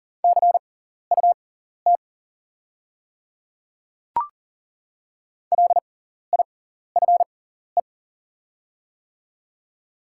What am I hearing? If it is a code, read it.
Morse code, heard as a keyed tone, about 37 wpm, spaced as for 14 wpm, 700 Hz. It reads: CUT LIFE
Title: 6x Character Spacing - Top 300 Words - 40wpm